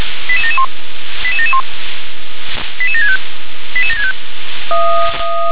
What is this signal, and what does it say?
Five-tone selective call (ZVEI), heard as quick runs of five tones, tones of about 70 ms. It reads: 88771 88771 88765 88765